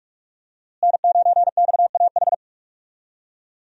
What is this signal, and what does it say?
Morse code, heard as a keyed tone, about 46 wpm, 700 Hz. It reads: N9XAH
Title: USA Call Signs - 5 Characters - Repeated 2x - 45wpm